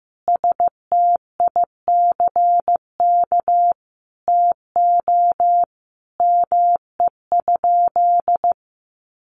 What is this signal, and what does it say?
Morse code, heard as a keyed tone, about 15 wpm, 700 Hz. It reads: STICK TO ME?
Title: Sentences from Top 1200 Words 15wpm